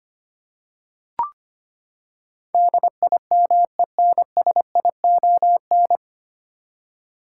Morse code, heard as a keyed tone, about 25 wpm, 700 Hz. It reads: DIMENSION
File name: Nine Letter Words - 25wpm